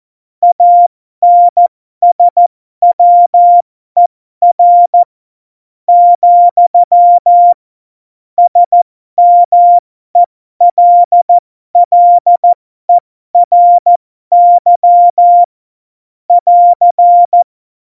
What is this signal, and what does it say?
Morse code, heard as a keyed tone, about 14 wpm, spaced as for 11 wpm, 700 Hz. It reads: ANSWER, SMELLERY <AR>
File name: Morsekurs / Witze